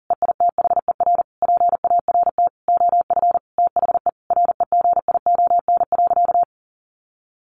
Morse code, heard as a keyed tone, about 40 wpm, 700 Hz. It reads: EITHER PART OF THE REGION.
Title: Sentences from Top 1500 Words 40wpm